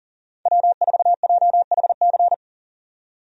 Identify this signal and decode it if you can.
Morse code, heard as a keyed tone, about 40 wpm, 700 Hz. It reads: W4JHC
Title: USA Call Signs - 5 Characters - 40wpm